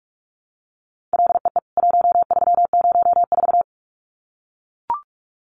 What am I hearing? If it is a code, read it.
Morse code, heard as a keyed tone, about 44 wpm, 700 Hz. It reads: LEE 1304
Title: CWT Exchange 45wpm